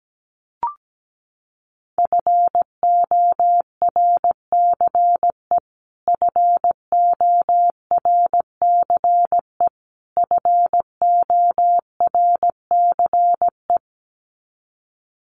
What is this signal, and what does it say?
Morse code, heard as a keyed tone, about 17 wpm, 700 Hz. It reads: FORCE FORCE FORCE
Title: Top 500 Words Repeated 17wpm